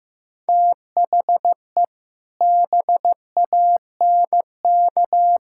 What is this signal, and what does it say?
Morse code, heard as a keyed tone, about 15 wpm, 700 Hz. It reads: THE BANK
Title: Sentences from Top 1000 Words - Review Part 5 - 15wpm